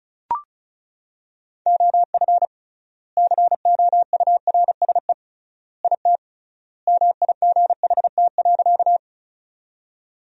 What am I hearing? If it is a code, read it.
Morse code, heard as a keyed tone, about 34 wpm, 700 Hz. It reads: OF COURSE IT MIGHT.
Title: 3x Word-Spacing - Sentences from Top 500 Words - 35wpm